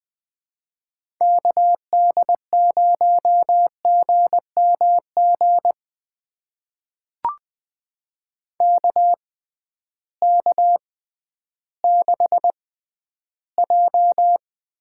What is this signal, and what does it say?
Morse code, heard as a keyed tone, about 20 wpm, spaced as for 12 wpm, 700 Hz. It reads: KD0GMG KK6J
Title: USA Call Signs - 6 Characters - 6x Character Spacing - 20wpm